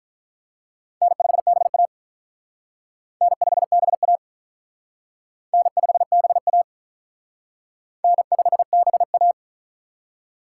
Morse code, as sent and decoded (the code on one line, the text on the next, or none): -. ..... -... .- / -. ..... -... .- / -. ..... -... .- / -. ..... -... .-
N5BA N5BA N5BA N5BA